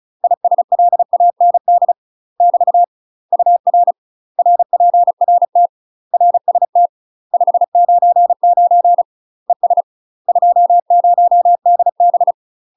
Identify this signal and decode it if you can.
Morse code, heard as a keyed tone, about 36 wpm, 700 Hz. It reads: ISLAND <BT> UR RPRT RST 599 ES 20DB
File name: UK MASTER QSOs TRAINING / 35WPM / UK MASTER QSOs 0016